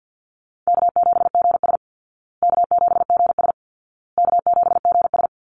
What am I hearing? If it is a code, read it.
Morse code, heard as a keyed tone, about 50 wpm, 700 Hz. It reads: K7GS K7GS K7GS
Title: Easy Call Signs Repeated 3x - 50wpm